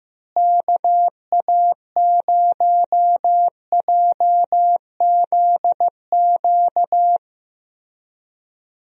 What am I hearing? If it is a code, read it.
Morse code, heard as a keyed tone, about 16 wpm, 700 Hz. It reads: KA0JZQ
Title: USA Call Signs - 6 Characters - Repeated 3x - 15wpm